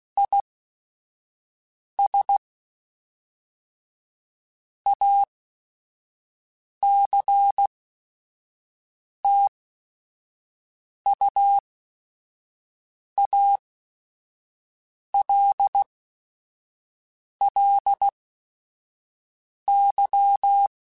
Morse code, heard as a keyed tone, about 16 wpm, spaced as for 5 wpm, 800 Hz. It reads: IS ACTUALLY
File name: Technical Subjects / Technical Subjects-mp3s / AND Gate @5